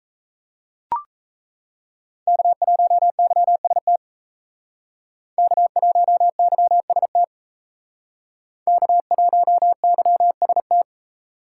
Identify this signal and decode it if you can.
Morse code, heard as a keyed tone, about 35 wpm, 700 Hz. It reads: K1YST K1YST K1YST